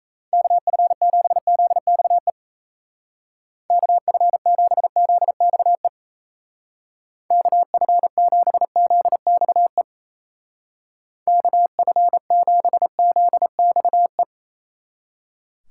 Morse code, heard as a keyed tone, about 33 wpm, 700 Hz. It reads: KF7ZXE KF7ZXE KF7ZXE KF7ZXE
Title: Speed Racing - USA 2x3 Call Signs 28wpm